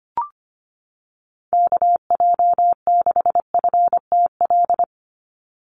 Morse code, heard as a keyed tone, about 25 wpm, 700 Hz. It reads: KJ6FTL
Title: USA Call Signs 25wpm